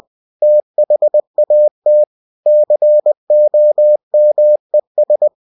Morse code, heard as a keyed tone, about 20 wpm, 600 Hz. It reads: THAT COMES
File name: cw-books / winnie-the-pooh / winnie-20wpm-page-0011